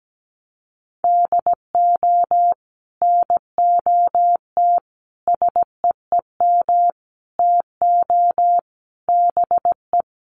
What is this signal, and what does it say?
Morse code, heard as a keyed tone, about 17 wpm, 700 Hz. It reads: DO NOT SEEM TO BE
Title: Sets of 5 Words Encore2 17wpm